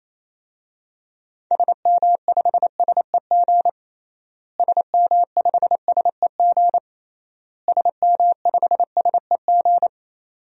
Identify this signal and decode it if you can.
Morse code, heard as a keyed tone, about 28 wpm, 700 Hz. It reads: SM5SEG SM5SEG SM5SEG
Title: DXCC-100 Call Signs - 6 Characters - Repeated 3x - 28wpm